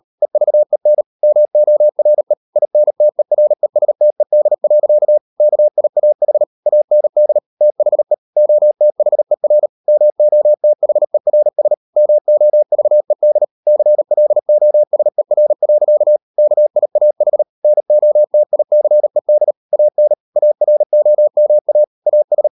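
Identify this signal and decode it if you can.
Morse code, heard as a keyed tone, about 38 wpm, 600 Hz. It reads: EVEN MORE INTERESTED. KIAH AND THE OTHER MOTHERS MOVED CLOSER. KIAH NOTICED AN AROMA AS
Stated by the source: Morse code practice at 38 WPM, sending Ken Follett's novel Never in English, from CW med Gnister.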